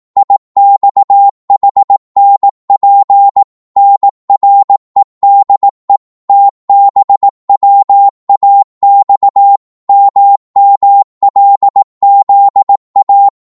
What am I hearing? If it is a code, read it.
Morse code, heard as a keyed tone, about 18 wpm, 800 Hz. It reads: IXHNP NREDE TBWAX MMLZA
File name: Morse / ltr-18wpm